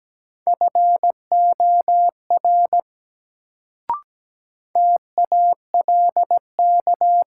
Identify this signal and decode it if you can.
Morse code, heard as a keyed tone, about 17 wpm, 700 Hz. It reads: FOR TALK
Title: Sets of 2 Words - New Word Drill - 17wpm